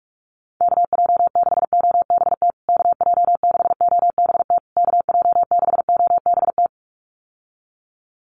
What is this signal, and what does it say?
Morse code, heard as a keyed tone, about 44 wpm, 700 Hz. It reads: KJ6OBT KJ6OBT KJ6OBT
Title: USA Call Signs Repeated 3x 45wpm